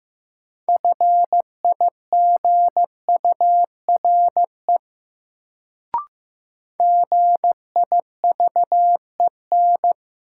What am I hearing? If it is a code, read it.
Morse code, heard as a keyed tone, about 15 wpm, 700 Hz. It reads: FIGURE GIVEN
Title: Sentences from Top 1400 Words - New Word Drill Alt - 15wpm